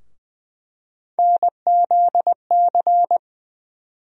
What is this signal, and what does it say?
Morse code, heard as a keyed tone, about 20 wpm, 700 Hz. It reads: NZC
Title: USA Call Sign Postfixes - 3 Letters 20wpm